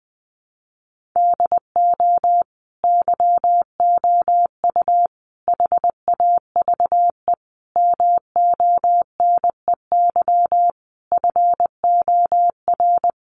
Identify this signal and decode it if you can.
Morse code, heard as a keyed tone, about 20 wpm, 700 Hz. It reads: DO YOU HAVE MONEY FOR
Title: Sentences from Top 1400 Words 20wpm